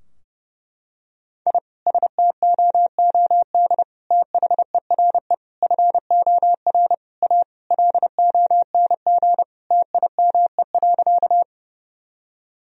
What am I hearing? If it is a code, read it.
Morse code, heard as a keyed tone, about 32 wpm, 700 Hz. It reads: I STOOD THERE FOR A LONG TIME.
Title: Sentences from Top 1000 Words - Review Part 2 - 30wpm